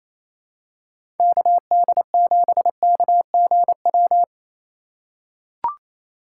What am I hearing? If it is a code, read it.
Morse code, heard as a keyed tone, about 28 wpm, 700 Hz. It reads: KD7KGW
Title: USA Call Signs - 4x Character Spacing - 28wpm